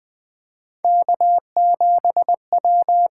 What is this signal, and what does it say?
Morse code, heard as a keyed tone, about 20 wpm, 700 Hz. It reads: K7W
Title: USA Call Signs - 8x Character Spacing - 20wpm